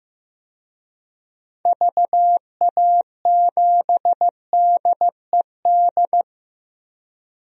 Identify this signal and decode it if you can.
Morse code, heard as a keyed tone, about 15 wpm, 700 Hz. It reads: VA7DED
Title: DXCC-100 Call Signs - 6 Characters - 15wpm